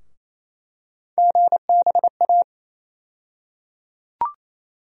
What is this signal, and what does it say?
Morse code, heard as a keyed tone, about 27 wpm, 700 Hz. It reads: GBA
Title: USA Call Sign Postfixes - 3 Letters 28wpm